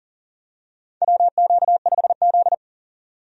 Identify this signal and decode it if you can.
Morse code, heard as a keyed tone, about 40 wpm, 700 Hz. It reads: WQ5Z
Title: USA Call Signs - 4 Characters - 40wpm